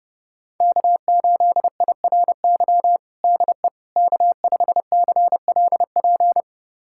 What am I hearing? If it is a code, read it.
Morse code, heard as a keyed tone, about 32 wpm, 700 Hz. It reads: K8IRY DE K5CLP
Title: USA Callsign DE Callsign - 5 Characters - 30wpm